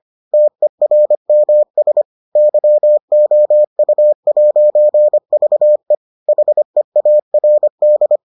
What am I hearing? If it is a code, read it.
Morse code, heard as a keyed tone, about 25 wpm, 600 Hz. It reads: TERMS YOU'VE HEARD